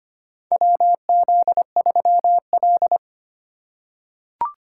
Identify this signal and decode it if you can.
Morse code, heard as a keyed tone, about 25 wpm, 700 Hz. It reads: WZ3L
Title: USA Call Signs - 4 Characters - 2x Character Spacing - 25wpm